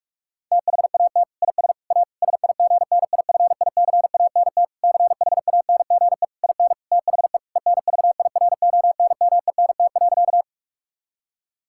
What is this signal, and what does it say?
Morse code, heard as a keyed tone, about 46 wpm, 700 Hz. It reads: THAT IS A SIGNIFICANT CHANGE IN THE ENVIRONMENT.
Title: Sentences from Top 1000 Words - Review Part 6 - 45wpm